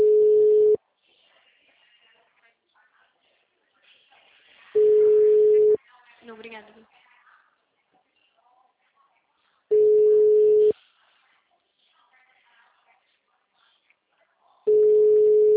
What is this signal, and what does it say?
Ringback tone (Europe, 425 Hz), with speech.